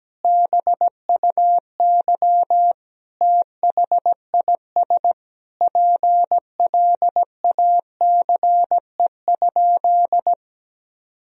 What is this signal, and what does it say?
Morse code, heard as a keyed tone, about 17 wpm, 700 Hz. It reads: BUY THIS PLACE?
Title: Sentences from Top 1000 Words - Review Part 4 - 17wpm